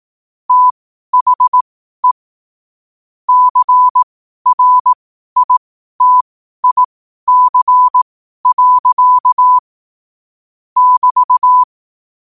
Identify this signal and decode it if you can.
Morse code, heard as a keyed tone, about 18 wpm, spaced as for 12 wpm, 1000 Hz. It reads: THE CRITIC. <BT>
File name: qotdfiles / 12wpm